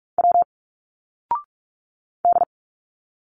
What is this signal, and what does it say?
Morse code, heard as a keyed tone, about 46 wpm, 700 Hz. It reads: W D
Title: Single Letters 45wpm